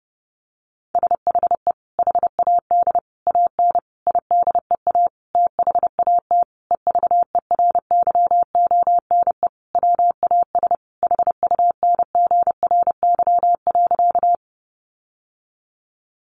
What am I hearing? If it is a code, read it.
Morse code, heard as a keyed tone, about 28 wpm, 700 Hz. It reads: SHE HAD AN IDEA THAT EVERYONE WAS HUNGRY.
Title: Sentences from Top 1400 Words 30wpm